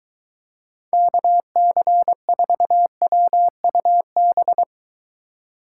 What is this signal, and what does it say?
Morse code, heard as a keyed tone, about 23 wpm, 700 Hz. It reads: KC4WUB